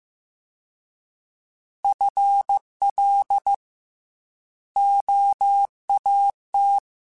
Morse code, heard as a keyed tone, about 15 wpm, 800 Hz. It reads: FL OAT
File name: FL / FL mp3s / Fast FL @13